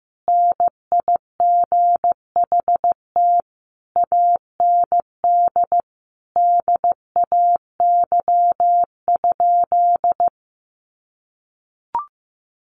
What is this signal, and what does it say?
Morse code, heard as a keyed tone, about 15 wpm, 700 Hz. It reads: NIGHT AND DAY?
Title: Sentences from Top 1100 Words 15wpm